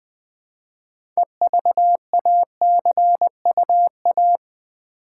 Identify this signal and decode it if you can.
Morse code, heard as a keyed tone, about 20 wpm, 700 Hz. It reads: EVACUA
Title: Build A Word - 3 Letters At A Time - 8-Letter Words - 20wpm